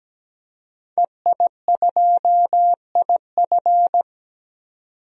Morse code, heard as a keyed tone, about 17 wpm, 700 Hz. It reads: EI2IF
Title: DXCC-100 Call Signs - 5 Characters - 17wpm